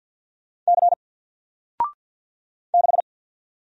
Morse code, heard as a keyed tone, about 48 wpm, 700 Hz. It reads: C 6
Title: Single Letter-Number Alt 50wpm